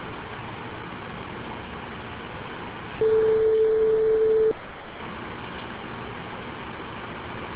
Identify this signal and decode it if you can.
Ringback tone (France), 440 Hz.